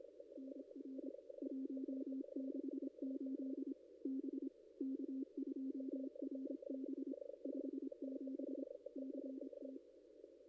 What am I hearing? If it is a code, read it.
Morse code, heard as a keyed tone, about 26 wpm, 270 Hz. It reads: NR 168 B K2RB 57 CT
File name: contests / ss-cw-k5tr-op-n5rz -1700